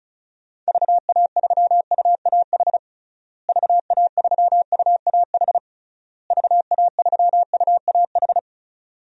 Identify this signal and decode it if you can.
Morse code, heard as a keyed tone, about 34 wpm, 700 Hz. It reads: VA3UAH VA3UAH VA3UAH
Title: DXCC-100 Call Signs - 6 Characters - Repeated 3x - 35wpm